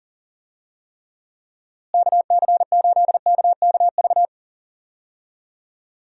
Morse code, heard as a keyed tone, about 40 wpm, 700 Hz. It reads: KC8KKV
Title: USA Call Signs - 6 Characters - Repeated 2x - 40wpm